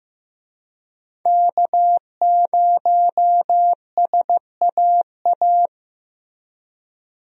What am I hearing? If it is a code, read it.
Morse code, heard as a keyed tone, about 15 wpm, 700 Hz. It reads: K0SAA
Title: USA Call Signs - 5 Characters - Repeated 3x - 15wpm